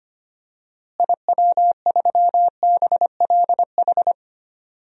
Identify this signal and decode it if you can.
Morse code, heard as a keyed tone, about 25 wpm, 700 Hz. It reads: IW3BLH
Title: DXCC-100 Call Signs - 6 Characters - Repeated 3x - 25wpm